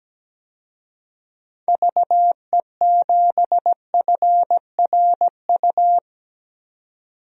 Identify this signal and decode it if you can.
Morse code, heard as a keyed tone, about 17 wpm, 700 Hz. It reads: VE7FRU
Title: DXCC-100 Call Signs - 6 Characters - 17wpm